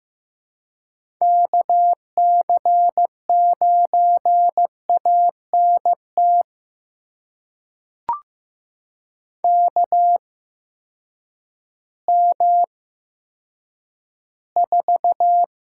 Morse code, heard as a keyed tone, about 15 wpm, spaced as for 9 wpm, 700 Hz. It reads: KC9ANT KM4
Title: USA Call Signs - 6 Characters - 8x Character Spacing - 15wpm